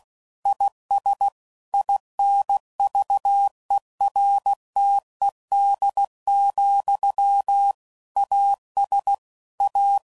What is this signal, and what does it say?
Morse code, heard as a keyed tone, about 16 wpm, 800 Hz. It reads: IS INVERTED, AS A
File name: CW Code Groups @15WPM / Technical Subjects / XC